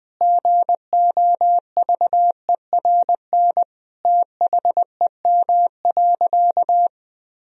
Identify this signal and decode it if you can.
Morse code, heard as a keyed tone, about 21 wpm, 700 Hz. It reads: GOVERN THEM.
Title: Sentences from Top 1000 Words - Review Part 2 - 20wpm